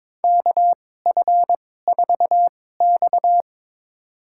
Morse code, heard as a keyed tone, about 22 wpm, spaced as for 16 wpm, 700 Hz. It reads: KF4X
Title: USA Call Signs - 4 Characters - 2x Character Spacing - 22wpm